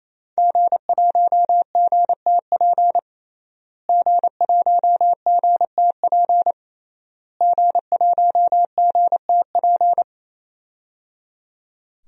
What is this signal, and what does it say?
Morse code, heard as a keyed tone, about 28 wpm, 700 Hz. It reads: G1GTP G1GTP G1GTP
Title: DXCC-100 Call Signs - 5 Characters - Repeated 3x - 28wpm